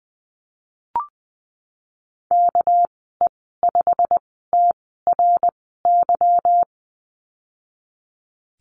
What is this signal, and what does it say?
Morse code, heard as a keyed tone, about 20 wpm, spaced as for 15 wpm, 700 Hz. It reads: KE5TRY
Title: 245 Lesson 39g - Letter z - Call Signs 20wpm